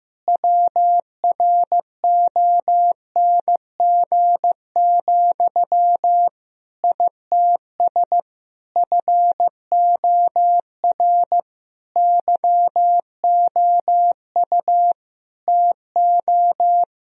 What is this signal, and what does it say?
Morse code, heard as a keyed tone, about 16 wpm, 700 Hz. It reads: WRONG, ITS FOR YOU TO
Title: Sentences from Top 1000 Words - Review Part 3 - 15wpm